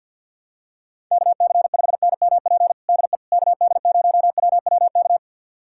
Morse code, heard as a keyed tone, about 52 wpm, 700 Hz. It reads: KC5NMP DE KD0WWK
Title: USA Callsign DE Callsign 50wpm